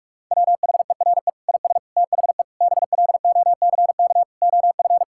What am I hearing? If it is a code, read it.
Morse code, heard as a keyed tone, about 41 wpm, 700 Hz. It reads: WHERE IS THE BLOCK OF
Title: Sentences from Top 1600 Words 45wpm